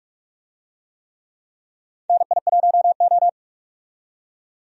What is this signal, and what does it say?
Morse code, heard as a keyed tone, about 41 wpm, 700 Hz. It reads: NI1O
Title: USA Call Signs - 4 Characters - 45wpm